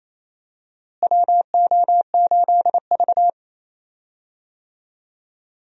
Morse code, heard as a keyed tone, about 28 wpm, 700 Hz. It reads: WO8V
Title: USA Call Signs - 4 Characters - Repeated 2x - 28wpm